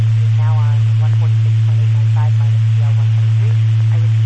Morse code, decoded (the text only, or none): none